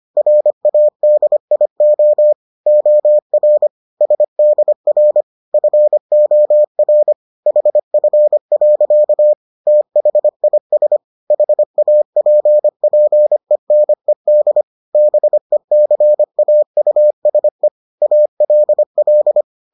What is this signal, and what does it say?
Morse code, heard as a keyed tone, about 25 wpm, 600 Hz. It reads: RADIO OR SDR FOR HF. THIS HAPPENED BECAUSE ALL